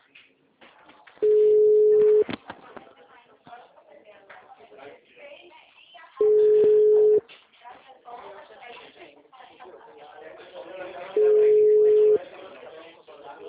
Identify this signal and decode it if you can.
Ringback tone (Europe, 425 Hz), with speech.